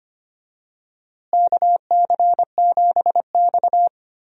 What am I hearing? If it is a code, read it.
Morse code, heard as a keyed tone, about 25 wpm, 700 Hz. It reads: KC7X
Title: USA Call Signs - 4 Characters - Repeated 3x - 25wpm